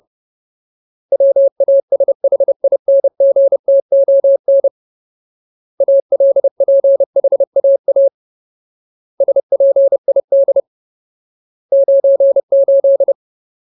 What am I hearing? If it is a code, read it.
Morse code, heard as a keyed tone, about 30 wpm, 550 Hz. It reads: WASHINGTON ALPHAA SPID 98
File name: Ultimate Word List / 30WPM Ultimate